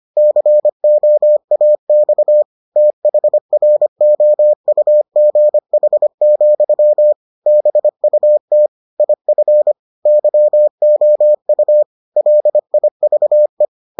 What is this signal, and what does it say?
Morse code, heard as a keyed tone, about 25 wpm, 600 Hz. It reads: COAX THROUGH, BUT IF YOU LIVE